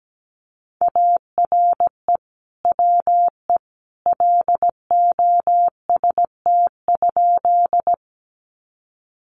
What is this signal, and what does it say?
Morse code, heard as a keyed tone, about 17 wpm, 700 Hz. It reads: ARE WE LOST?
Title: Sentences from Top 1400 Words 17wpm